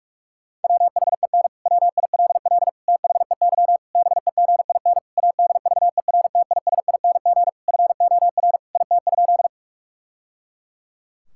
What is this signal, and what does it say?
Morse code, heard as a keyed tone, about 41 wpm, 700 Hz. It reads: WHEN WILL THEY BEGIN ADVERTISING FOR IT?
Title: Sentences from Top 1000 Words - Review Part 6 - 45wpm